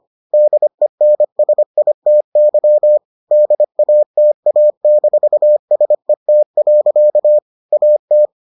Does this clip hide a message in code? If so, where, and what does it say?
Yes: Morse code from 0.3 to 8.3 s, heard as a keyed tone, about 25 wpm, 600 Hz. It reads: DENSITY DATA-SET. AT